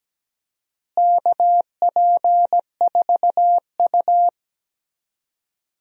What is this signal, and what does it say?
Morse code, heard as a keyed tone, about 17 wpm, 700 Hz. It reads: KP4U
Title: DXCC-100 Call Signs - 4 Characters - Repeated 2x - 17wpm